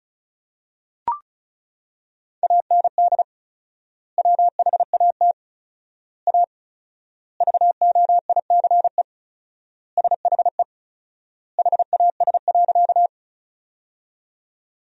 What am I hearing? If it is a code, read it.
Morse code, heard as a keyed tone, about 34 wpm, 700 Hz. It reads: AND WHAT A VOICE SHE HAS.
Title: 4x Word-Spacing - Sentences from Top 500 Words - 35wpm